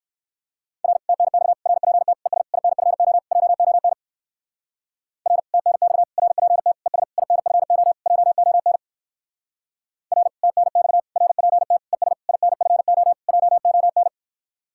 Morse code, heard as a keyed tone, about 66 wpm, 700 Hz. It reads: R TNX RPT ES INFO JON R TNX RPT ES INFO JON R TNX RPT ES INFO JON